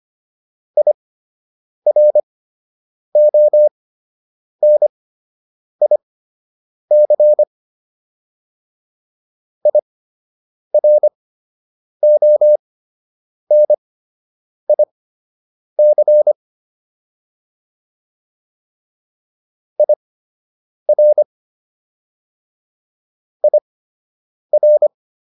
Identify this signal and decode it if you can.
Morse code, heard as a keyed tone, about 25 wpm, spaced as for 8 wpm, 600 Hz. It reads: IRONIC IRONIC IR IR